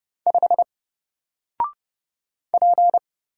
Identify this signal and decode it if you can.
Morse code, heard as a keyed tone, about 30 wpm, 700 Hz. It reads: 5 P
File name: Single-Letter-Number-30wpm